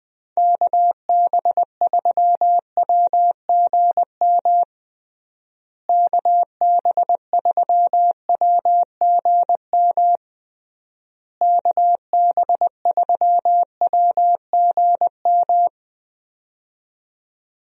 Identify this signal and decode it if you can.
Morse code, heard as a keyed tone, about 20 wpm, 700 Hz. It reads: KB3WGM KB3WGM KB3WGM
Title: USA Call Signs - 6 Characters - Repeated 3x - 20wpm